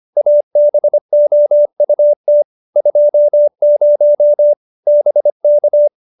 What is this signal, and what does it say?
Morse code, heard as a keyed tone, about 25 wpm, 570 Hz. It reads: ABOUT 20 BK